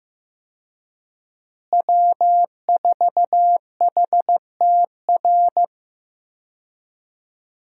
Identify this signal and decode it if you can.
Morse code, heard as a keyed tone, about 15 wpm, 700 Hz. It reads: W4HTR
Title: USA Call Signs - 5 Characters - Repeated 3x - 15wpm